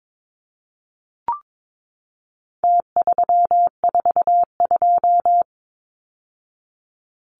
Morse code, heard as a keyed tone, about 22 wpm, 700 Hz. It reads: T342
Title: Numbers - Contest Style - 4-Digit 22wpm